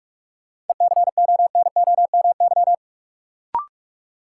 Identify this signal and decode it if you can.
Morse code, heard as a keyed tone, about 41 wpm, 700 Hz. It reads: ECONOMY
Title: Sentences from Top 1200 Words - New Word Drill Alt - 45wpm